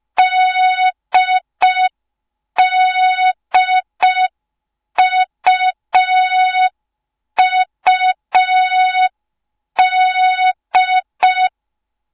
Morse code, heard as a keyed tone, about 5 wpm, 3700 Hz. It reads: DDUUD